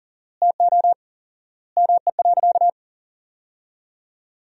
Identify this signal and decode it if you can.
Morse code, heard as a keyed tone, about 40 wpm, 700 Hz. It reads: TO ME.
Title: 4x Word-Spacing - Sentences from Top 200 Words - 40wpm